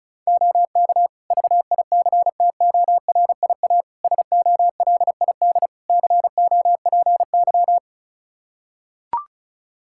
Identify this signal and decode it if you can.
Morse code, heard as a keyed tone, about 36 wpm, 700 Hz. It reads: OK VICTORIA SOLID COPY
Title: Practice QSO - Exchange 3 - More Info - 35wpm